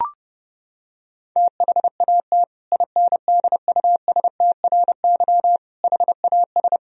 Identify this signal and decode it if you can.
Morse code, heard as a keyed tone, about 32 wpm, 700 Hz. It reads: THAT INDUSTRY HAS
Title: Sentences from Top 1000 Words - Review Part 4 - 30wpm